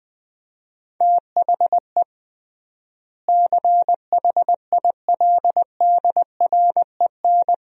Morse code, heard as a keyed tone, about 20 wpm, 700 Hz. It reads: THE CHILDREN
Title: 3x Word-Spacing - Sentences from Top 300 Words - 20wpm